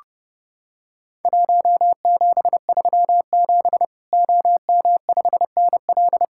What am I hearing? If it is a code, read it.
Morse code, heard as a keyed tone, about 32 wpm, 700 Hz. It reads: 1737 OM5NL
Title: Worked All Europe DX Contest - QTC Exchange - 30wpm